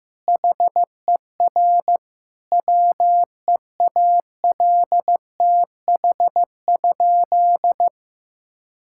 Morse code, heard as a keyed tone, about 15 wpm, 700 Hz. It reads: HER WEALTH?
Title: Sentences from Top 1600 Words 15wpm